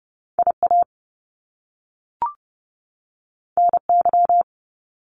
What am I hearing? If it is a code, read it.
Morse code, heard as a keyed tone, about 30 wpm, 700 Hz. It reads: IA NY
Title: US States 30wpm